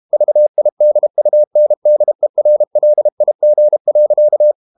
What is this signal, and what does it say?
Morse code, heard as a keyed tone, about 32 wpm, 600 Hz. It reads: VIDUNDERLIG.
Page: Ferie 32 wpm